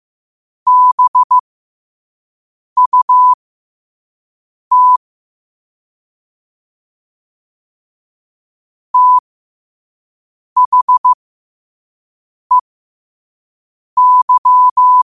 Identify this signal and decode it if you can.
Morse code, heard as a keyed tone, about 15 wpm, spaced as for 5 wpm, 1000 Hz. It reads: BUT THEY
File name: qotdfiles / 5wpm